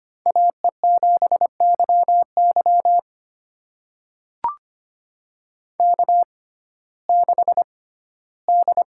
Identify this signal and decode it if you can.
Morse code, heard as a keyed tone, about 25 wpm, spaced as for 15 wpm, 700 Hz. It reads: AE7YY K6D
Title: USA Call Signs - 6x Character Spacing - 25wpm